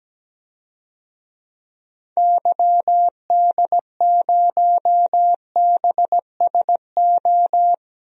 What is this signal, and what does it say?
Morse code, heard as a keyed tone, about 17 wpm, 700 Hz. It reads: YD0BSO
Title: DXCC-100 Call Signs - 6 Characters - Repeated 3x - 17wpm